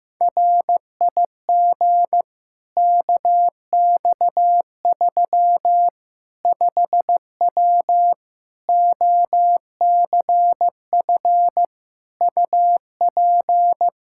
Morse code, heard as a keyed tone, about 15 wpm, 700 Hz. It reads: RIG KX3 5W OCF UP
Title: Practice QSO - Exchange 2 - Info - 15wpm